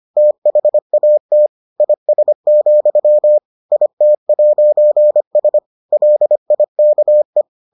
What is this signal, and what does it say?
Morse code, heard as a keyed tone, about 25 wpm, 600 Hz. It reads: THAT IS, IT'S LIKE